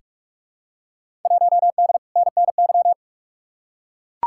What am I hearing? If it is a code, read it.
Morse code, heard as a keyed tone, about 44 wpm, 700 Hz. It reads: 1D NNY
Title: Field Day Exchanges - 45wpm